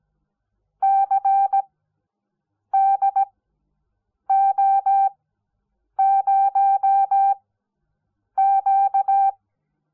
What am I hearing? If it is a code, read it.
Morse code, heard as a keyed tone, about 17 wpm, spaced as for 7 wpm, 780 Hz. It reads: CDO0Q
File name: Onlinelehrgang / morsen / DJ4UF Morsen05